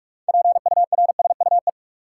Morse code, heard as a keyed tone, about 46 wpm, 700 Hz. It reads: PURSUE
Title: Six Letter Words - 45wpm